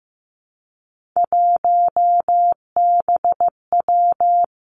Morse code, heard as a keyed tone, about 15 wpm, 700 Hz. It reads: 1BW
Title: Three Challenging Letter-Number Combinations 15wpm